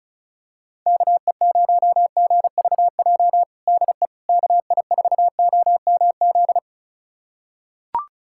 Morse code, heard as a keyed tone, about 36 wpm, 700 Hz. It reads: KE0GVJ DE KI4OMZ
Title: USA Callsign DE Callsign 35wpm